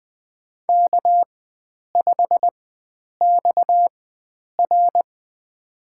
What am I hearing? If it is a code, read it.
Morse code, heard as a keyed tone, about 20 wpm, spaced as for 9 wpm, 700 Hz. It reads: K5XR
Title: USA Call Signs - 4 Characters - 4x Character Spacing - 20wpm